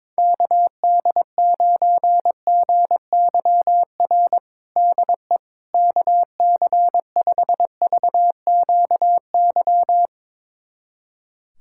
Morse code, heard as a keyed tone, about 22 wpm, 700 Hz. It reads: KD9GYR DE KC5VQY
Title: USA Callsign DE Callsign - 6 Characters - 22wpm